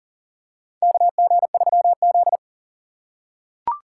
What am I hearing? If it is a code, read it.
Morse code, heard as a keyed tone, about 40 wpm, 700 Hz. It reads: KG3Z
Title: USA Call Signs - 4 Characters - 3x Character Spacing - 40wpm